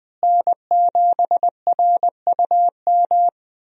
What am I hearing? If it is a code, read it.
Morse code, heard as a keyed tone, about 21 wpm, 700 Hz. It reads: N7RUM
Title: USA Call Signs - 5 Characters - Repeated 2x - 20wpm